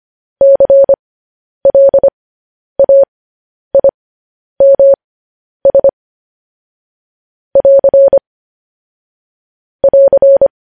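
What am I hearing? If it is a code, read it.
Morse code, heard as a keyed tone, about 25 wpm, spaced as for 10 wpm, 550 Hz. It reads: CLAIMS <AR> <AR>
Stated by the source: Morse code transcription of news headlines.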